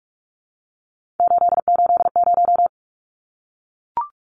Morse code, heard as a keyed tone, about 44 wpm, 700 Hz. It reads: 880
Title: Numbers - 3-Digit 45wpm